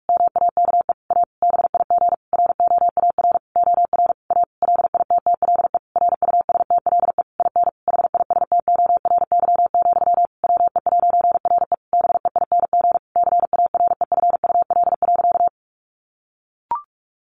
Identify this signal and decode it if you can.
Morse code, heard as a keyed tone, about 44 wpm, 700 Hz. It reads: MAKE A BIG ROAR OR A LITTLE RUSTLE IN HISTORY, WE'RE BEING CAREFUL.
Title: A Sound Of Thunder 45wpm